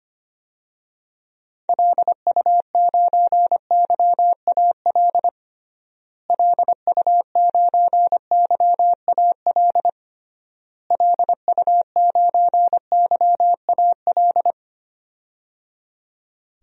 Morse code, heard as a keyed tone, about 25 wpm, 700 Hz. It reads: LU9YAL LU9YAL LU9YAL
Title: DXCC-100 Call Signs - 6 Characters - Repeated 3x - 25wpm